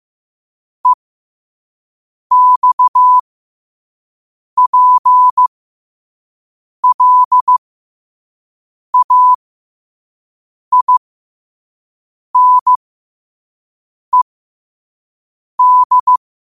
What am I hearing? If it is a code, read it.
Morse code, heard as a keyed tone, about 15 wpm, spaced as for 5 wpm, 1000 Hz. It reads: EXPLAINED